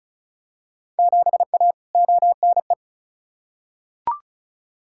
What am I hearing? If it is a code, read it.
Morse code, heard as a keyed tone, about 36 wpm, 690 Hz. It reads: 7A ONE
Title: Field Day Exchanges Encore - 35wpm